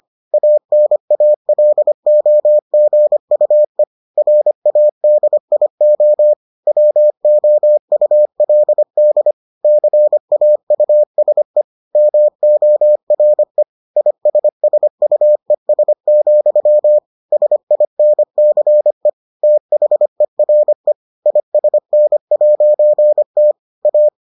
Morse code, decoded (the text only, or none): ANALOGUE RADIO WOULD CAUSE MORE ISSUES, SINCE THERE ISN'T A